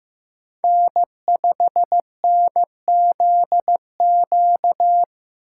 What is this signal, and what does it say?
Morse code, heard as a keyed tone, about 15 wpm, 700 Hz. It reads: N5NZQ
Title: USA Call Signs - 5 Characters - 4x Character Spacing - 15wpm